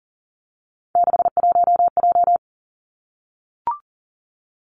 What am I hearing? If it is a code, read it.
Morse code, heard as a keyed tone, about 42 wpm, 700 Hz. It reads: 61J
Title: Three Challenging Letter-Number Combinations 40wpm